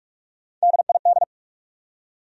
Morse code, heard as a keyed tone, about 59 wpm, 700 Hz. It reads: DID